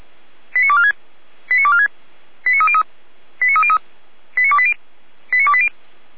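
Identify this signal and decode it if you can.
Five-tone selective call (ZVEI), heard as quick runs of five tones, tones of about 80 ms. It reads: 89246 89246 89303 89303 89280 89280
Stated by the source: ZVEI five-tone selective call received by SDR, Feuerwehr Orlamünde fire brigade.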